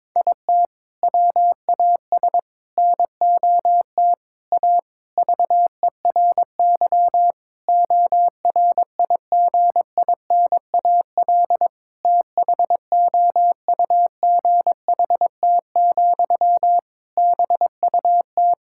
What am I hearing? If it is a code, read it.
Morse code, heard as a keyed tone, about 22 wpm, 700 Hz. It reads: IT WAS NOT A VERY ORIGINAL THOUGHT, BUT
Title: Sentences from Top 1000 Words - Review Part 6 - 22wpm